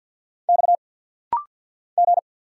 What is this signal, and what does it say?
Morse code, heard as a keyed tone, about 50 wpm, 700 Hz. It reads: X G
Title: Single Letter-Number Alt - Rapid-Fire 50wpm